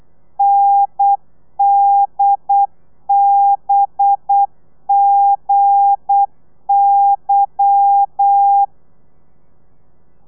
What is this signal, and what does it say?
Morse code, heard as a keyed tone, about 8 wpm, 800 Hz. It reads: NDBGY
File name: telegrafia / i2tfj / Lez11 35